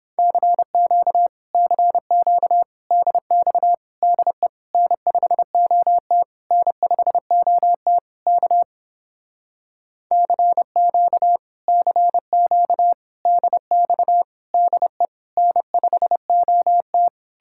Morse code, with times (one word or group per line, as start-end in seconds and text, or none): 0.2-1.3 CQ
1.5-2.6 CQ
2.9-3.7 DX
4.0-4.5 DE
4.7-6.2 N5OT
6.5-8.0 N5OT
8.3-8.6 K
10.1-11.4 CQ
11.7-12.9 CQ
13.3-14.2 DX
14.5-15.1 DE
15.4-17.1 N5OT